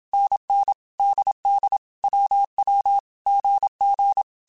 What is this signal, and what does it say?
Morse code, heard as a keyed tone, about 27 wpm, 800 Hz. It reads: NN DD WW GG
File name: Basic Sets / Basic Letter Groups Collective7 / AWUNDG@25